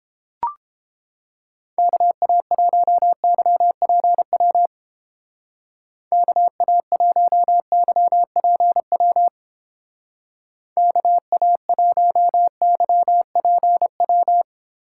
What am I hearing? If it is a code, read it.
Morse code, heard as a keyed tone, about 30 wpm, 700 Hz. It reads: KA1YPW KA1YPW KA1YPW